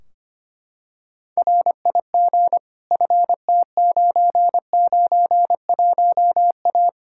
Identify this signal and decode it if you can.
Morse code, heard as a keyed tone, about 25 wpm, 700 Hz. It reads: RIG FT991A
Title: K1USN SST QSO Practice 25wpm